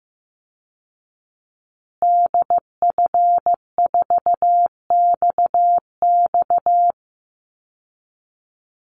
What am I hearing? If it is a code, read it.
Morse code, heard as a keyed tone, about 15 wpm, 700 Hz. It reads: DF4XX
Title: Call Signs Repeated 3x - 15wpm